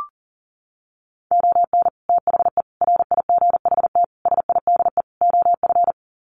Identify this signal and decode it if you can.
Morse code, heard as a keyed tone, about 40 wpm, 700 Hz. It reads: ON THE RIGHT SIDE OF
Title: Sets of 5 Words Encore 40wpm